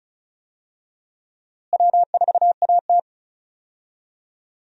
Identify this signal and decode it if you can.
Morse code, heard as a keyed tone, about 36 wpm, 700 Hz. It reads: W4AT
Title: USA Call Signs - 4 Characters - Repeated 2x - 35wpm